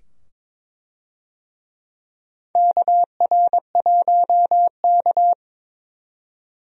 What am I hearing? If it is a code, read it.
Morse code, heard as a keyed tone, about 22 wpm, 700 Hz. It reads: KR1K
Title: USA Call Signs - 4 Characters - 22wpm